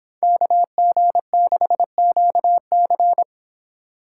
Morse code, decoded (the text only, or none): KG6QC